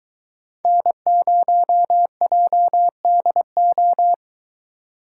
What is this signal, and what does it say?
Morse code, heard as a keyed tone, about 23 wpm, 700 Hz. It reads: N0JDO